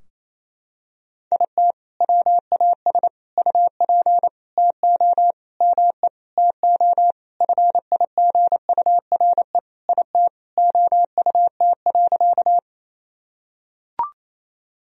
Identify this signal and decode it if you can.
Morse code, heard as a keyed tone, about 28 wpm, 700 Hz. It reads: IT WAS UP TO ME TO FIGURE IT OUT.
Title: Sentences from Top 1000 Words - Review Part 2 - 28wpm